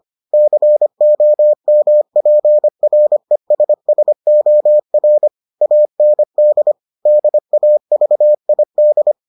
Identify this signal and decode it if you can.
Morse code, heard as a keyed tone, about 25 wpm, 600 Hz. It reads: COMPRESSOR AND DAVID